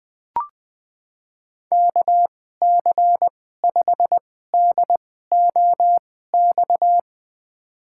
Morse code, heard as a keyed tone, about 20 wpm, spaced as for 14 wpm, 700 Hz. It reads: KC5DOX
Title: USA Call Signs - 6 Characters - 2x Character Spacing - 20wpm